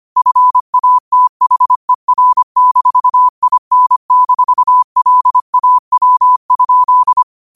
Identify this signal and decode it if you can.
Morse code, heard as a keyed tone, about 25 wpm, 1000 Hz. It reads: FATHER-IN-LAW?
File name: qotdfiles / 25wpm